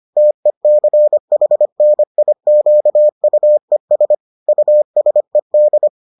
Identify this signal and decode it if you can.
Morse code, heard as a keyed tone, about 25 wpm, 600 Hz. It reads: TECHNIQUES USED